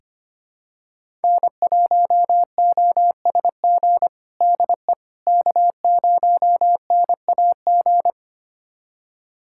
Morse code, heard as a keyed tone, about 25 wpm, 700 Hz. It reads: N1OSG DE K0NAG
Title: USA Callsign DE Callsign - 5 Characters - 25wpm